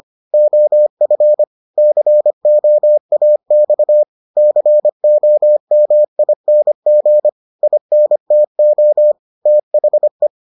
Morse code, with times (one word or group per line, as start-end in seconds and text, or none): 0.3-1.4 OF
1.8-4.0 COAX
4.4-7.3 COMING
7.6-9.1 INTO
9.5-10.3 THE